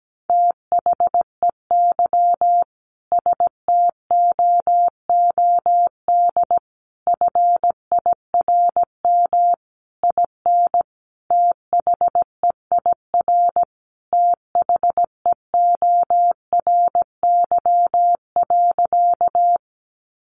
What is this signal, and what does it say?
Morse code, heard as a keyed tone, about 17 wpm, 700 Hz. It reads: THEY STOOD FIRM IN THEIR THEORY.
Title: Sentences from Top 1400 Words 17wpm